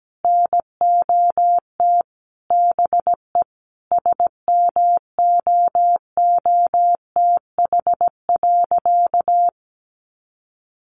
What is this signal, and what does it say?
Morse code, heard as a keyed tone, about 17 wpm, 700 Hz. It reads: NOT BE SMOOTH.
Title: Sentences from Top 1400 Words 17wpm